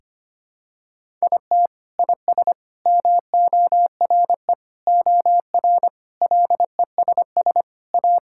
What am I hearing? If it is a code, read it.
Morse code, heard as a keyed tone, about 25 wpm, 700 Hz. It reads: IT IS MORE OR LESS A
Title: Sentences from Top 1000 Words - Review Part 6 - 25wpm